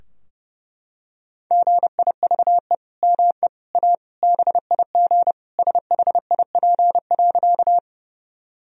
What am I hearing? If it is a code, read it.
Morse code, heard as a keyed tone, about 30 wpm, 700 Hz. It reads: GIVE ME A BIG SHIP.
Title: Sentences from Top 1000 Words - Review Part 2 - 30wpm